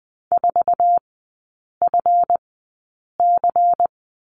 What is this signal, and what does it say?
Morse code, heard as a keyed tone, about 20 wpm, spaced as for 8 wpm, 700 Hz. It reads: 4FC
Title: 138 Lesson 23e - Letter f - Three Characters 20wpm